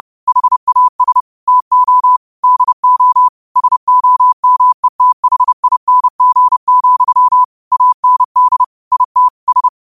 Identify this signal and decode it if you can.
Morse code, heard as a keyed tone, about 30 wpm, 1000 Hz. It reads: HAS TO DO SOMETHING, AND IT S